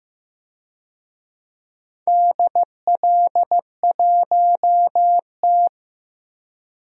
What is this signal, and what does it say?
Morse code, heard as a keyed tone, about 15 wpm, 700 Hz. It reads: DL1T
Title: DXCC-100 Call Signs - 4 Characters - 15wpm